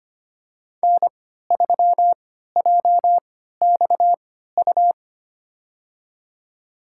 Morse code, heard as a keyed tone, about 25 wpm, spaced as for 14 wpm, 700 Hz. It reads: N3JXU
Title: USA Call Signs - 3x Character Spacing - 25wpm